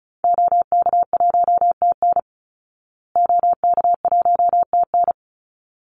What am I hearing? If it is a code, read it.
Morse code, heard as a keyed tone, about 34 wpm, 700 Hz. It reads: OK1TN OK1TN
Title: Call Signs Repeated 3x - 35wpm